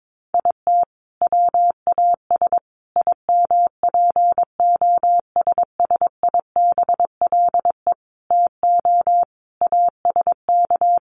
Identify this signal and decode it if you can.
Morse code, heard as a keyed tone, about 22 wpm, 700 Hz. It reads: IT WAS IMPOSSIBLE TO ASK
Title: Sentences from Top 1100 Words 22wpm